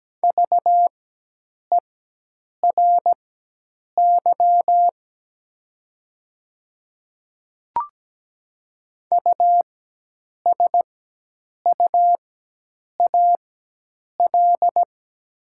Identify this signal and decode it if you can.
Morse code, heard as a keyed tone, about 17 wpm, spaced as for 8 wpm, 700 Hz. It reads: VERY USUAL
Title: 4x Character Spacing - Top 500 Words - 17wpm